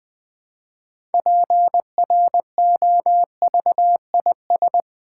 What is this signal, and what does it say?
Morse code, heard as a keyed tone, about 20 wpm, 700 Hz. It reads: PROVIS
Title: Build A Word - 3 Letters At A Time - 9-Letter Words - 20wpm